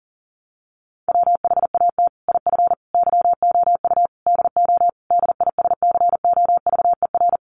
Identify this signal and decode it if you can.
Morse code, heard as a keyed tone, about 40 wpm, 700 Hz. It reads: WHAT IF YOU DO DISCOVER
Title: Sentences from Top 1100 Words 40wpm